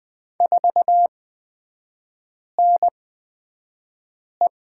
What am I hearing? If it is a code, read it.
Morse code, heard as a keyed tone, about 20 wpm, spaced as for 5 wpm, 700 Hz. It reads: 4NE
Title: 229 Lesson 37e - Letter x - Three Characters 20wpm